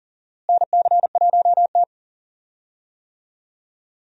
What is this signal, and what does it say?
Morse code, heard as a keyed tone, about 39 wpm, 700 Hz. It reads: NC1T
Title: USA Call Signs - 4 Characters - Repeated 3x - 40wpm